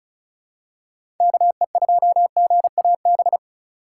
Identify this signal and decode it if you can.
Morse code, heard as a keyed tone, about 36 wpm, 700 Hz. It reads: KE2GAB
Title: USA Call Signs - 4x Character Spacing - 35wpm